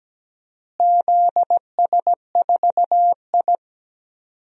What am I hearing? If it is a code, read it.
Morse code, heard as a keyed tone, about 17 wpm, 700 Hz. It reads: ZS4I
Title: DXCC-100 Call Signs - 4 Characters - Repeated 2x - 17wpm